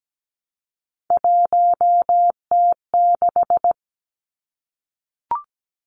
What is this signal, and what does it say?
Morse code, heard as a keyed tone, about 17 wpm, 700 Hz. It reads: 1T6
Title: Three Challenging Letter-Number Combinations 17wpm